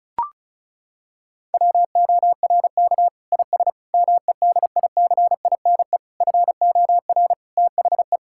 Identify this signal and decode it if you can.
Morse code, heard as a keyed tone, about 36 wpm, 700 Hz. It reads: WORK IS MEDICINE FOR THE
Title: Sentences from Top 1000 Words - Review Part 6 - 35wpm